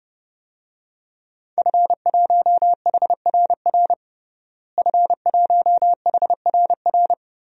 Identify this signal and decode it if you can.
Morse code, heard as a keyed tone, about 32 wpm, 700 Hz. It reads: F1HRR F1HRR
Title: DXCC-100 Call Signs - 5 Characters - Repeated 3x - 30wpm